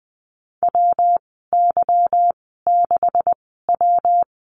Morse code, heard as a keyed tone, about 20 wpm, spaced as for 15 wpm, 700 Hz. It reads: WY6W
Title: 147 Lesson 24g - Letter y - Call Signs 20wpm